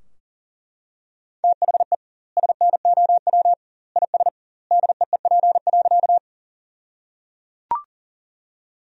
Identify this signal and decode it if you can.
Morse code, heard as a keyed tone, about 37 wpm, 700 Hz. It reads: THE SNOW IS DEEP.
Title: 2x Word-Spacing - Sentences from Top 500 Words - 40wpm